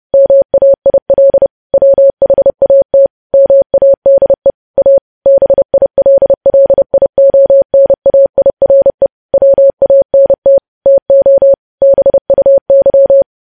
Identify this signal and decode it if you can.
Morse code, heard as a keyed tone, about 30 wpm, 550 Hz. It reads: MAIL WHAT MADE A BILLIONAIRE WANT TO BUY
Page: news headlines in morse code at 30 wpm